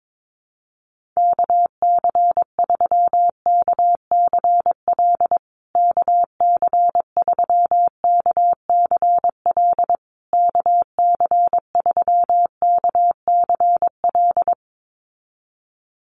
Morse code, heard as a keyed tone, about 22 wpm, 700 Hz. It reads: KC3KCL KC3KCL KC3KCL
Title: USA Call Signs Repeated 3x 22wpm